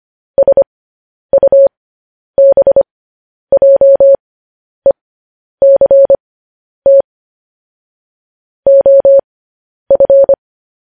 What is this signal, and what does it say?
Morse code, heard as a keyed tone, about 25 wpm, spaced as for 10 wpm, 550 Hz. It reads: SUBJECT OF
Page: news headlines in morse code at 10 wpm